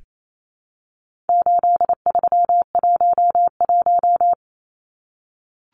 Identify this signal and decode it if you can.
Morse code, heard as a keyed tone, about 28 wpm, 700 Hz. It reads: 8311
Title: Numbers - 4-Digit 28wpm